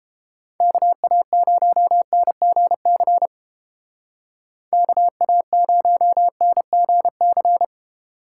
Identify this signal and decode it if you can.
Morse code, heard as a keyed tone, about 31 wpm, 700 Hz. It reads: KA0NGC KA0NGC